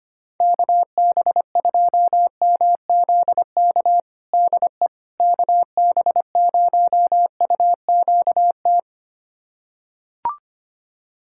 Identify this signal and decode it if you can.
Morse code, heard as a keyed tone, about 25 wpm, 700 Hz. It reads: KB2MZK DE KB0UQT
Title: USA Callsign DE Callsign 25wpm